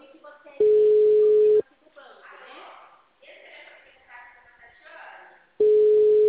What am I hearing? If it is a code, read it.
Ringback tone (Europe, 425 Hz), with speech.